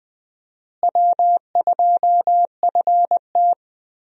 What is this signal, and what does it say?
Morse code, heard as a keyed tone, about 20 wpm, 700 Hz. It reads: W2FT
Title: USA Call Signs - 4 Characters - Repeated 3x - 20wpm